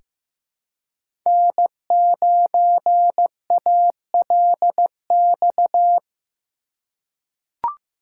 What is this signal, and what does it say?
Morse code, heard as a keyed tone, about 15 wpm, 700 Hz. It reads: N9ALX
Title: USA Call Signs - 5 Characters - 4x Character Spacing - 15wpm